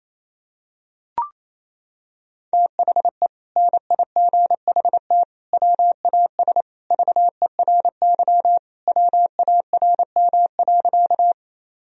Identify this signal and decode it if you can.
Morse code, heard as a keyed tone, about 28 wpm, 700 Hz. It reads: THE NIGHT WAS VERY WARM.
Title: Sentences from Top 1000 Words - Review Part 2 - 28wpm